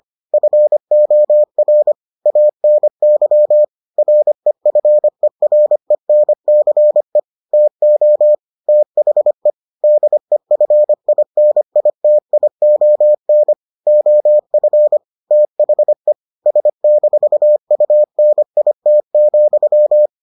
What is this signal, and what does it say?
Morse code, heard as a keyed tone, about 25 wpm, 600 Hz. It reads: FOR ANY REFERENCE TO THE DEFINITION OF THE S-UNIT,